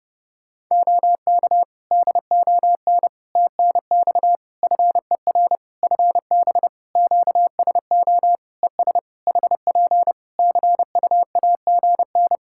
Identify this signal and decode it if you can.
Morse code, heard as a keyed tone, about 30 wpm, 700 Hz. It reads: OK DON TNX FER FB QSO ES HP CUAGN
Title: Practice QSO - Exchange 4 - Ending - 30wpm